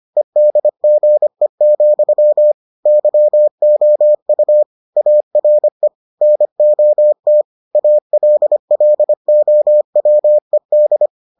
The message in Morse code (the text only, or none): EDGE, YOU ARE NOT ALLOWED